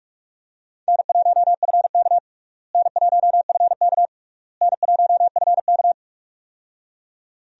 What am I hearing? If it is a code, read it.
Morse code, heard as a keyed tone, about 46 wpm, 700 Hz. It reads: N1FK N1FK N1FK
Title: USA Call Signs - 4 Characters - Repeated 3x - 45wpm